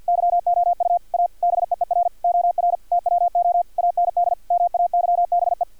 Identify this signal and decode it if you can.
Morse code, heard as a keyed tone, about 50 wpm, 700 Hz, with radio noise. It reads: YOU A BEER OR TWO AND MAYBE
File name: archived / CRSnet